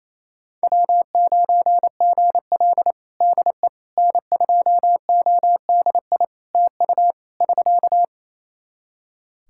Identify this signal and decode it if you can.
Morse code, heard as a keyed tone, about 28 wpm, 700 Hz. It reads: W9GL DE N2ODI TU <SK>
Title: Practice QSO - Exchange 4 - Ending - 28wpm